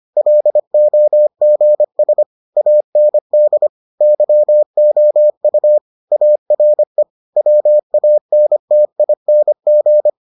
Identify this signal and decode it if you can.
Morse code, heard as a keyed tone, about 25 wpm, 600 Hz. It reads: LOGS AND YOU ARE WANTING